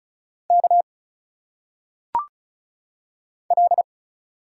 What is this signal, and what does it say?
Morse code, heard as a keyed tone, about 34 wpm, 700 Hz. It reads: K L
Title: Single Letters - Phonetic Pronunciation 35wpm